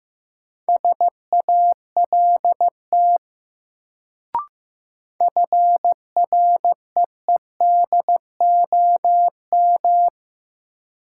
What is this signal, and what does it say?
Morse code, heard as a keyed tone, about 15 wpm, 700 Hz. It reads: SALT FREEDOM
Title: Sentences from Top 800 Words - New Word Drill - 15wpm